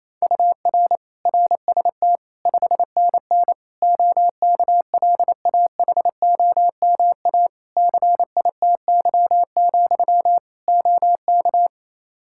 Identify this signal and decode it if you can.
Morse code, heard as a keyed tone, about 28 wpm, 700 Hz. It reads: UR RST 5NN OKLAHOMA CITY, OK
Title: Practice QSO - Exchange 1 - Beginning - 28wpm